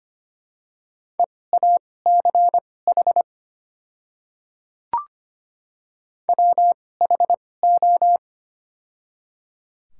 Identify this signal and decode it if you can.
Morse code, heard as a keyed tone, about 25 wpm, spaced as for 18 wpm, 690 Hz. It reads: EACH WHO
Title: 2x Character Spacing - Top 100 Words - 25wpm